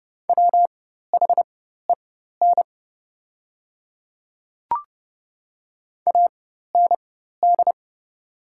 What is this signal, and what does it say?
Morse code, heard as a keyed tone, about 30 wpm, spaced as for 14 wpm, 690 Hz. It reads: WHEN AND
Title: 4x Character Spacing - Top 100 Words - 30wpm